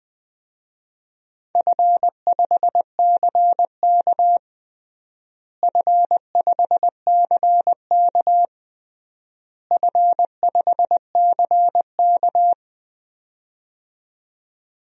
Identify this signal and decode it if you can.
Morse code, heard as a keyed tone, about 21 wpm, 700 Hz. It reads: F5CK F5CK F5CK
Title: DXCC-100 Call Signs - 4 Characters - Repeated 3x - 20wpm